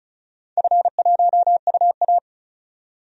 Morse code, heard as a keyed tone, about 36 wpm, 700 Hz. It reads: F1UA
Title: DXCC-100 Call Signs - 4 Characters - 35wpm